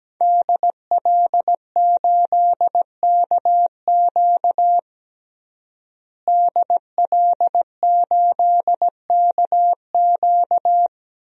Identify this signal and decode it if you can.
Morse code, heard as a keyed tone, about 17 wpm, 700 Hz. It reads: DL8KQ DL8KQ
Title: DXCC-100 Call Signs - 5 Characters - Repeated 3x - 17wpm